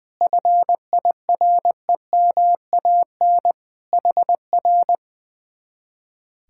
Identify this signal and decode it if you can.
Morse code, heard as a keyed tone, about 19 wpm, 700 Hz. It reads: FIREMAN HR
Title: Speed-Racing Common QSO Phrases - 20wpm